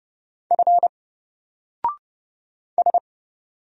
Morse code, heard as a keyed tone, about 30 wpm, 700 Hz. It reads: F S
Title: Single Letter-Number Alt 30wpm